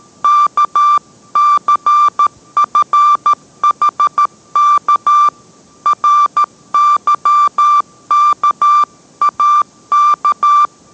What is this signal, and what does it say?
Morse code, heard as a keyed tone, about 13 wpm, 1180 Hz. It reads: KCFHK RYKAK